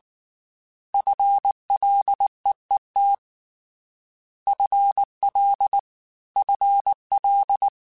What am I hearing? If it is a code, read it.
Morse code, heard as a keyed tone, about 19 wpm, 800 Hz. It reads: FLEET FL FL
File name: FL / FL mp3s / FL Prefixes 3x2 Twice @18